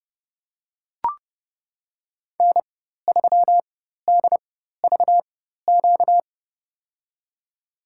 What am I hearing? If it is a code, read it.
Morse code, heard as a keyed tone, about 32 wpm, spaced as for 14 wpm, 700 Hz. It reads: N3DVQ
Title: USA Call Signs - 5 Characters - 4x Character Spacing - 30wpm